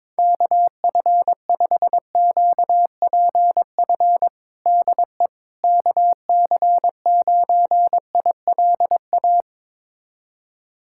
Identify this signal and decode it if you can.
Morse code, heard as a keyed tone, about 22 wpm, 700 Hz. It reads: KF5QPF DE KC9ILA
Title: USA Callsign DE Callsign 22wpm